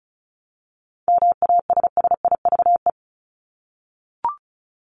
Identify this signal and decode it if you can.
Morse code, heard as a keyed tone, about 34 wpm, 700 Hz. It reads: MASSIVE
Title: Top 1751-2000 Words 35wpm